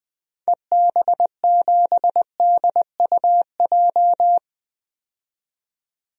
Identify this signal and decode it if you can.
Morse code, heard as a keyed tone, about 20 wpm, 700 Hz. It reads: EB7DUJ
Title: DXCC-100 Call Signs - 6 Characters - 20wpm